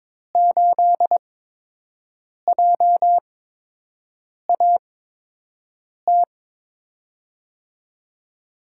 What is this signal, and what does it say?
Morse code, heard as a keyed tone, about 22 wpm, spaced as for 6 wpm, 700 Hz. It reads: 8JAT
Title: USA Call Signs - 6 Characters - 8x Character Spacing - 22wpm